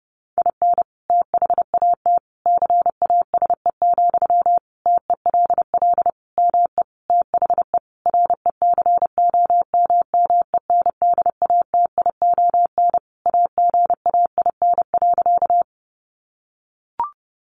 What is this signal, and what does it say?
Morse code, heard as a keyed tone, about 30 wpm, 700 Hz. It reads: IN THAT CASE, TELL ME THE RECOMMENDATION AGAIN.
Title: Sentences from Top 1400 Words 30wpm